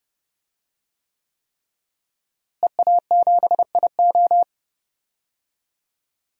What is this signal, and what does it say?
Morse code, heard as a keyed tone, about 30 wpm, 700 Hz. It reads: EA7IO
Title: DXCC-100 Call Signs - 5 Characters - Repeated 3x - 30wpm